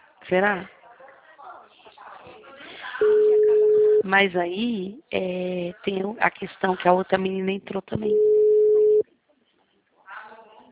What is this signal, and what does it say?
Ringback tone (Europe, 425 Hz), with speech.